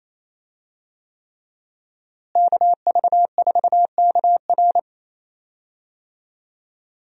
Morse code, heard as a keyed tone, about 27 wpm, 700 Hz. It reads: KV4KR
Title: USA Call Signs - 5 Characters - Repeated 3x - 28wpm